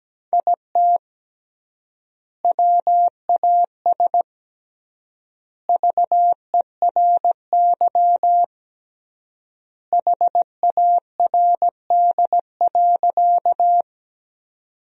Morse code, heard as a keyed tone, about 17 wpm, 700 Hz. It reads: IT WAS VERY HARD.
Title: 3x Word-Spacing - Sentences from Top 300 Words - 17wpm